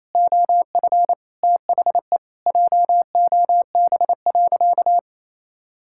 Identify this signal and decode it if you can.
Morse code, heard as a keyed tone, about 28 wpm, 700 Hz. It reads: OF THE JOB.
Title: Sentences from Top 1000 Words - Review Part 5 - 28wpm